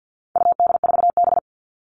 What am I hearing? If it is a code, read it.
Morse code, heard as a keyed tone, about 50 wpm, 700 Hz. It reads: UD4B
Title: All Cut Numbers - 4-Digit 50wpm